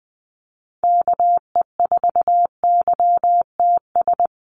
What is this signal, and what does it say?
Morse code, heard as a keyed tone, about 20 wpm, 700 Hz. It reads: KE4YTS
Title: USA Call Signs Repeated 3x 20wpm